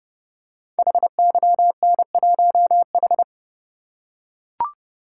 Morse code, heard as a keyed tone, about 28 wpm, 700 Hz. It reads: HYN1H
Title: Code Groups - 5 Letter:Numbers - 30wpm